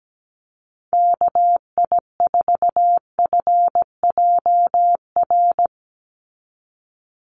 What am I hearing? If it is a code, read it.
Morse code, heard as a keyed tone, about 17 wpm, 700 Hz. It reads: KI4FJR
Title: USA Call Signs 17wpm